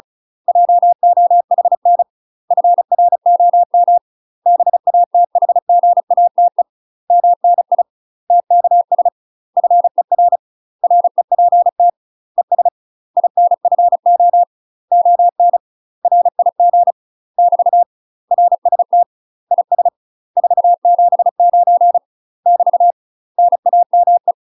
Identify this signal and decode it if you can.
Morse code, heard as a keyed tone, about 34 wpm, 700 Hz. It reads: JOHN FROM BATHGATE MNI TKS FER REPT ES INFO ON RIG <BT> RST IS 479 <BT> NAME
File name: UK MASTER QSOs TRAINING / 35WPM / UK MASTER QSOs 0039